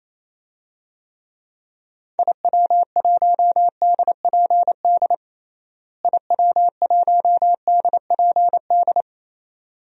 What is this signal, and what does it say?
Morse code, heard as a keyed tone, about 28 wpm, 700 Hz. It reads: IW1DPD IW1DPD
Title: DXCC-100 Call Signs - 6 Characters - Repeated 2x - 28wpm